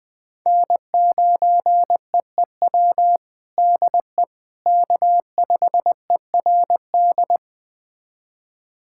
Morse code, heard as a keyed tone, about 21 wpm, 700 Hz. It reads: N9EEW DE K5ERD
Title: USA Callsign DE Callsign - 5 Characters - 20wpm